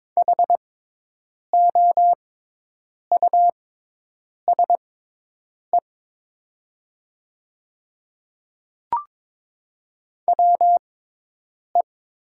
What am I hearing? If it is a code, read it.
Morse code, heard as a keyed tone, about 22 wpm, spaced as for 8 wpm, 700 Hz. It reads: HOUSE WE
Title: 6x Character Spacing - Top 300 Words - 22wpm